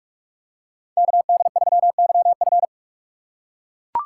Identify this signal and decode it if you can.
Morse code, heard as a keyed tone, about 43 wpm, 700 Hz. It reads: KD3YF